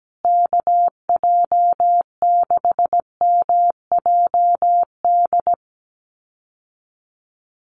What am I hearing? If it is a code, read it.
Morse code, heard as a keyed tone, about 17 wpm, 700 Hz. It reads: KJ6MJD
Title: USA Call Signs Repeated 3x 17wpm